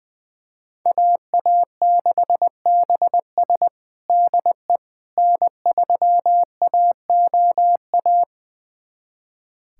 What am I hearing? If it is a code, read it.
Morse code, heard as a keyed tone, about 20 wpm, 700 Hz. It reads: AA6BS DE N3AOA
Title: USA Callsign DE Callsign - 5 Characters - 20wpm